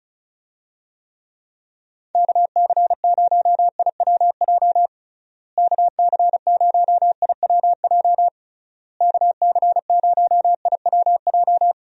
Morse code, heard as a keyed tone, about 34 wpm, 700 Hz. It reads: KC0IWJ KC0IWJ KC0IWJ
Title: USA Call Signs - 6 Characters - Repeated 3x - 35wpm